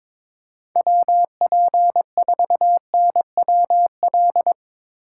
Morse code, heard as a keyed tone, about 22 wpm, 700 Hz. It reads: WP4NWL
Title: Speed Racing - USA 2x3 Call Signs 22wpm